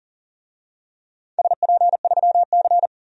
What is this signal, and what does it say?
Morse code, heard as a keyed tone, about 39 wpm, 700 Hz. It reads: SP3C
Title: DXCC-100 Call Signs - 4 Characters - 40wpm